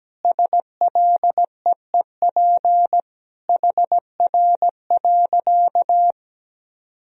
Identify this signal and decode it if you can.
Morse code, heard as a keyed tone, about 17 wpm, 700 Hz. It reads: SLEEP HR.
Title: Sentences from Top 500 Words and Abbreviations 17wpm